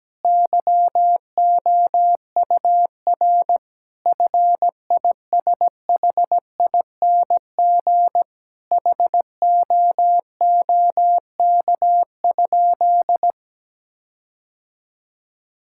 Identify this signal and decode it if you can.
Morse code, heard as a keyed tone, about 17 wpm, 700 Hz. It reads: YOUR FISHING HOOK?
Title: Sentences from Top 1600 Words 17wpm